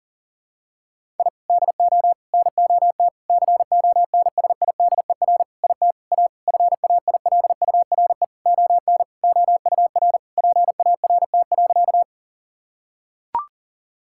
Morse code, heard as a keyed tone, about 37 wpm, 700 Hz. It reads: I DO NOT CONSIDER IT A FAILURE ON OUR PART.
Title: Sentences from Top 1000 Words - Review Part 6 - 40wpm